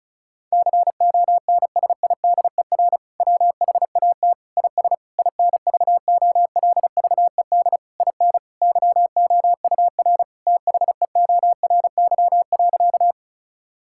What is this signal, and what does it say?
Morse code, heard as a keyed tone, about 36 wpm, 700 Hz. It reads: CONSIDER WHAT IS INVOLVED IN YOUR THEORY.
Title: Sentences from Top 1000 Words - Review Part 5 - 35wpm